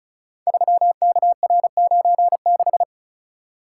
Morse code, heard as a keyed tone, about 36 wpm, 700 Hz. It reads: 3KR96
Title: Code Groups - 5 Letter:Numbers - 35wpm